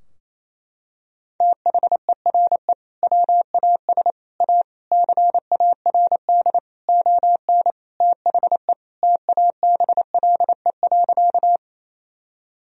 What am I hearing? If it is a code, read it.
Morse code, heard as a keyed tone, about 28 wpm, 700 Hz. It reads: THERE WAS A CARD ON THE TABLE.
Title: Sentences from Top 1000 Words - Review Part 4 - 28wpm